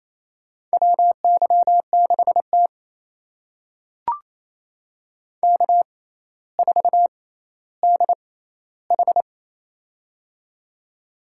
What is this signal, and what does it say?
Morse code, heard as a keyed tone, about 28 wpm, spaced as for 14 wpm, 700 Hz. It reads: WY6T K4DH
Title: USA Call Signs - 4 Characters - 6x Character Spacing - 28wpm